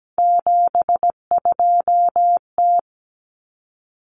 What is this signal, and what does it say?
Morse code, heard as a keyed tone, about 17 wpm, 700 Hz. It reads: 72T
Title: Common Cut Numbers - 3-Digit 17wpm